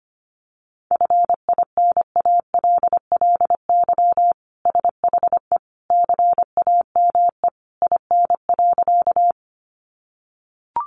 Morse code, heard as a keyed tone, about 25 wpm, 700 Hz. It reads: FINALLY SHE CAME IN.
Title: Sentences from Top 1000 Words 25wpm